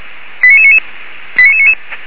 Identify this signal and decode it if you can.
Five-tone selective call (ZVEI), heard as quick runs of five tones, tones of about 70 ms. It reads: 89999 89999
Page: ZVEI